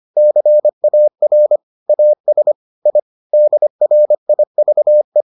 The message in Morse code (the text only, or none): CAR AS I DRIVE